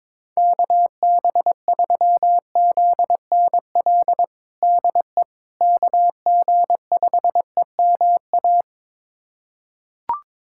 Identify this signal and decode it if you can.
Morse code, heard as a keyed tone, about 22 wpm, 700 Hz. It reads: KB3ZNL DE KG5EMA
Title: USA Callsign DE Callsign - 6 Characters - 22wpm